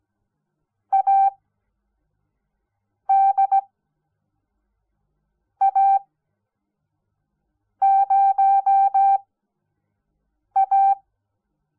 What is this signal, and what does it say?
Morse code, heard as a keyed tone, about 17 wpm, spaced as for 4 wpm, 780 Hz. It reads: ADA0A
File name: Onlinelehrgang / morsen / DJ4UF Morsen06